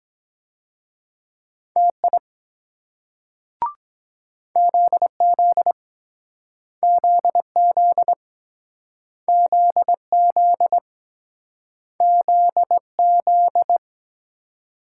Morse code, heard as a keyed tone, about 23 wpm, 700 Hz. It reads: TI ZZ ZZ ZZ ZZ